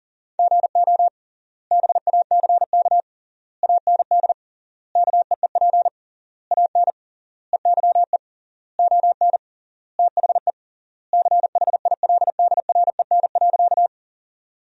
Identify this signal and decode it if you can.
Morse code, heard as a keyed tone, about 40 wpm, 700 Hz. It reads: GO BACK AND KEEP AN EYE ON THE CHILDREN.
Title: 3x Word-Spacing - Sentences from Top 300 Words - 40wpm